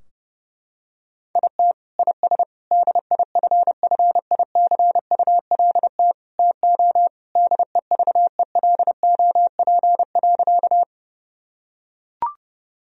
Morse code, heard as a keyed tone, about 32 wpm, 700 Hz. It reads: IT IS DIFFICULT TO DEVELOP.
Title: Sentences from Top 1000 Words - Review Part 2 - 30wpm